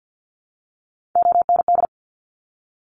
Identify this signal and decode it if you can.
Morse code, heard as a keyed tone, about 50 wpm, 700 Hz. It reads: OND
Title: Top Three Letter Combinations in Words 50wpm